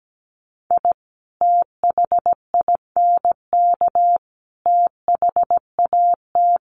Morse code, heard as a keyed tone, about 17 wpm, 700 Hz. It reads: I THINK THAT
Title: Sets of 5 Words Encore2 17wpm